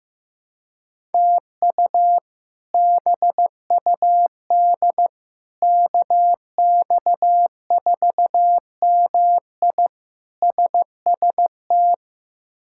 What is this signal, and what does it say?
Morse code, heard as a keyed tone, about 15 wpm, 700 Hz. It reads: TU BUD KX4MI SST
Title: K1USN SST QSO Practice - 15wpm